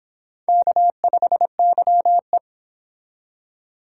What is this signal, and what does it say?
Morse code, heard as a keyed tone, about 26 wpm, 700 Hz. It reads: K5YE